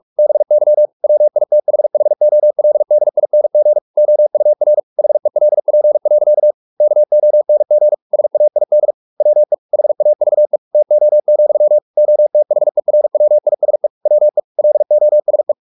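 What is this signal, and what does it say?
Morse code, heard as a keyed tone, about 44 wpm, 600 Hz. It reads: BY WITHHOLDING OUR HELP. KONG SAID WE HAVE TO, OTHERWISE WE LOSE
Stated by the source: Morse practice at 45 wpm, the novel Never by Ken Follett in English, from CW med Gnister.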